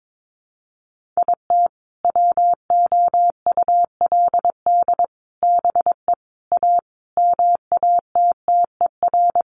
Morse code, heard as a keyed tone, about 22 wpm, 700 Hz. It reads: IT WOULD BE A MATTER
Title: Sentences from Top 1400 Words 22wpm